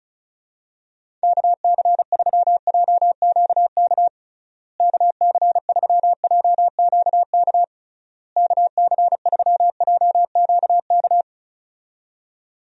Morse code, heard as a keyed tone, about 34 wpm, 700 Hz. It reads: KC3JQK KC3JQK KC3JQK
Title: USA Call Signs - 6 Characters - Repeated 3x - 35wpm